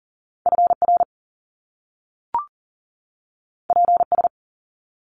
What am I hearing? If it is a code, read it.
Morse code, heard as a keyed tone, about 42 wpm, 700 Hz. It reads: FR PS
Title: Top Two Letter Combinations in Words 40wpm